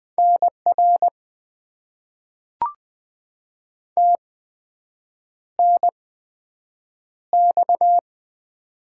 Morse code, heard as a keyed tone, about 20 wpm, spaced as for 7 wpm, 700 Hz. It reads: NR TNX
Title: Top 90 QSO Elements - 8x Character Spacing - 20wpm